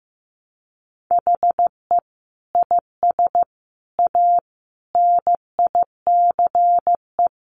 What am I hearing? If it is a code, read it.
Morse code, heard as a keyed tone, about 15 wpm, 700 Hz. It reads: HE IS A NICE
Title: Sentences from Top 1100 Words 15wpm